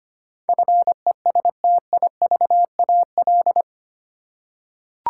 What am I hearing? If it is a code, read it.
Morse code, heard as a keyed tone, about 25 wpm, 700 Hz. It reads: FESTIVAL
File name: 8 Letter Words - First 3 Letters Given - 25wpm